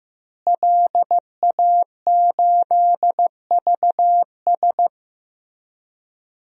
Morse code, heard as a keyed tone, about 16 wpm, 700 Hz. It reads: LA8VS
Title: DXCC-100 Call Signs - 5 Characters - Repeated 3x - 15wpm